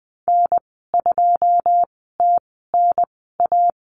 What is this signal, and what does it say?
Morse code, heard as a keyed tone, about 20 wpm, spaced as for 14 wpm, 700 Hz. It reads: N2TNA
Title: 082 Lesson 14g - Number 2 - Call Signs 20wpm